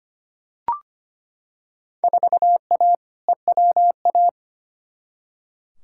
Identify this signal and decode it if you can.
Morse code, heard as a keyed tone, about 25 wpm, 700 Hz. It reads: 4A EWA
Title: Field Day Exchanges - 25wpm